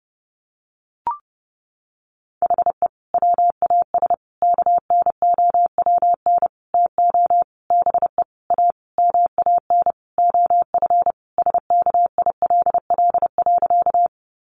Morse code, heard as a keyed tone, about 30 wpm, 700 Hz. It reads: HE WAS KNOWN TO BE A MAN OF SKILL.
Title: Sentences from Top 1200 Words 30wpm